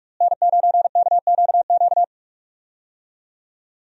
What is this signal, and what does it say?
Morse code, heard as a keyed tone, about 42 wpm, 700 Hz. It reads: N9KQQ
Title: USA Call Signs - 5 Characters - Repeated 2x - 45wpm